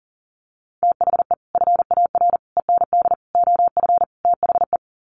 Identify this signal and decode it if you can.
Morse code, heard as a keyed tone, about 37 wpm, 700 Hz. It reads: THE FAR END OF THE
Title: Sets of 5 Words 40wpm